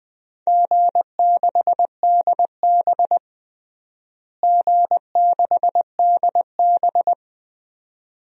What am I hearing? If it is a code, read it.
Morse code, heard as a keyed tone, about 20 wpm, 700 Hz. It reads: G6DB G6DB
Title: DXCC-100 Call Signs - 4 Characters - Repeated 2x - 20wpm